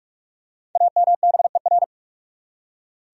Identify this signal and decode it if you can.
Morse code, heard as a keyed tone, about 43 wpm, 700 Hz. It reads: AMBER
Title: USA Common Names - Max 5 Letters - 45wpm